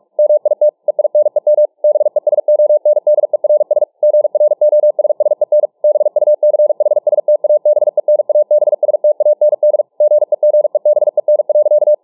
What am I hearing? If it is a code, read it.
Morse code, heard as a keyed tone, about 47 wpm, 600 Hz. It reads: MIT EINEM BESONDERS GROSSEN BUCHSTABENABSTAND GEGEBEN.